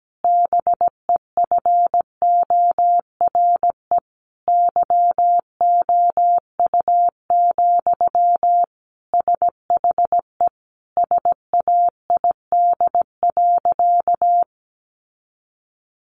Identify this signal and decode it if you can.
Morse code, heard as a keyed tone, about 17 wpm, 700 Hz. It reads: BEFORE YOU, SHE SAID.
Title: Sentences from Top 400 Words 17wpm